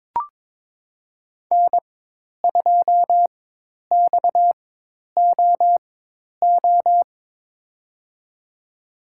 Morse code, heard as a keyed tone, about 23 wpm, spaced as for 10 wpm, 700 Hz. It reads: N2XOO
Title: USA Call Signs - 5 Characters - 4x Character Spacing - 22wpm